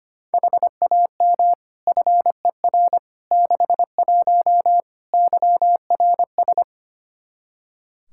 Morse code, heard as a keyed tone, about 25 wpm, 700 Hz. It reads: HAM FER 61 YRS
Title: Practice QSO - Exchange 3 - More Info - 25wpm